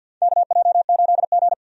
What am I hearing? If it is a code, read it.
Morse code, heard as a keyed tone, about 52 wpm, 700 Hz. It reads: KJ8G
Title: USA Call Signs - 4 Characters - 50wpm